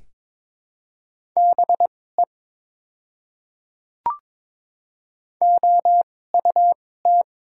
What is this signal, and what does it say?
Morse code, heard as a keyed tone, about 22 wpm, spaced as for 16 wpm, 700 Hz. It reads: BE OUT
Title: 2x Character Spacing - Top 100 Words - 22wpm